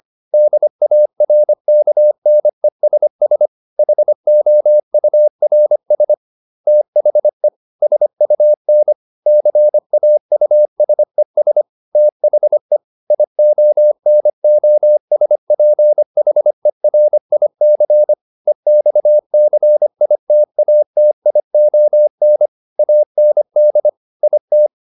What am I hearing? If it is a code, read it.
Morse code, heard as a keyed tone, about 25 wpm, 600 Hz. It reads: DARKNESS HOURS THE SUN CAUSES THE IONOSPHERIC EXCITATION AND IT
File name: SS211 25